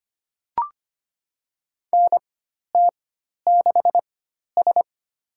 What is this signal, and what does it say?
Morse code, heard as a keyed tone, about 25 wpm, spaced as for 12 wpm, 700 Hz. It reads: NT6S
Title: USA Call Signs - 4x Character Spacing - 25wpm